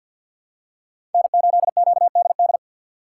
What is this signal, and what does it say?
Morse code, heard as a keyed tone, about 53 wpm, 700 Hz. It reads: N8QDD
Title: USA Call Signs - 5 Characters - 6x Character Spacing - 50wpm